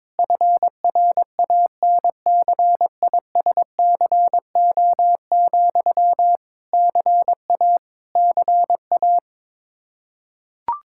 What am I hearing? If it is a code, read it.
Morse code, heard as a keyed tone, about 22 wpm, 700 Hz. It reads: FRANCISCO, CA CA
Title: Practice QSO - Exchange 1 - Beginning - 22wpm